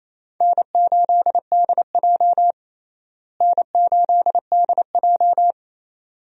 Morse code, heard as a keyed tone, about 27 wpm, 700 Hz. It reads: N8DJ N8DJ
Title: USA Call Signs - 4 Characters - Repeated 2x - 28wpm